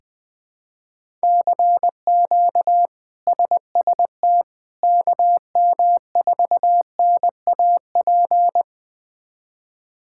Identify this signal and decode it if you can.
Morse code, heard as a keyed tone, about 20 wpm, 700 Hz. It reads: CQ SST KM4NAP
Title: K1USN SST QSO Practice - 20wpm